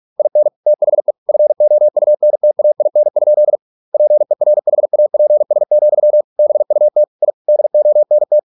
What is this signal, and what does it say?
Morse code, heard as a keyed tone, about 46 wpm, 600 Hz. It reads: IN THE FOUNTAIN? PERHAPS, BUT I DONT